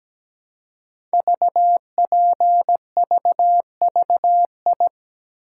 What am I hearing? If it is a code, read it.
Morse code, heard as a keyed tone, about 17 wpm, 700 Hz. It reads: VPVVI
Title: Code Groups - 5 Letters - 17wpm